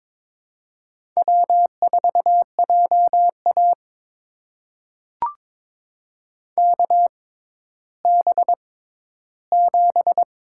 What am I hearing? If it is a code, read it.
Morse code, heard as a keyed tone, about 22 wpm, spaced as for 12 wpm, 700 Hz. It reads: W4JA KB7
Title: USA Call Signs - 6x Character Spacing - 22wpm